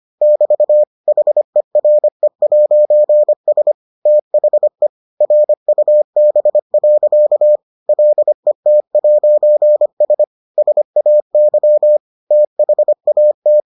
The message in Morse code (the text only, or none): <BT> HERE'S THE RUB. LET'S SAY THAT